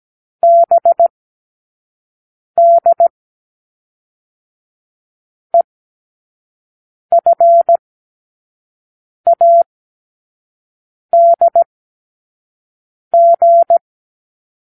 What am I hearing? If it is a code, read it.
Morse code, heard as a keyed tone, about 17 wpm, 680 Hz. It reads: B D E F A D G
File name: k7qo / Lesson15(A-J)